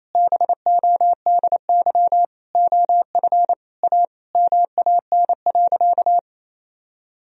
Morse code, heard as a keyed tone, about 28 wpm, 700 Hz. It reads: BODY OF A MAN.
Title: Sentences from Top 1000 Words - Review Part 2 - 28wpm